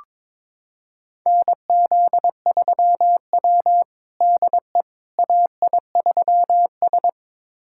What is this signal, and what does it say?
Morse code, heard as a keyed tone, about 22 wpm, 700 Hz. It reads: NZ3W DE AI3S
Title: USA Callsign DE Callsign - 4 Characters - 22wpm